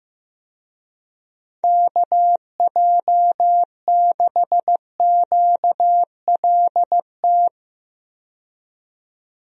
Morse code, heard as a keyed tone, about 15 wpm, 700 Hz. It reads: KJ6QLT
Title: USA Call Signs - 6 Characters - 15wpm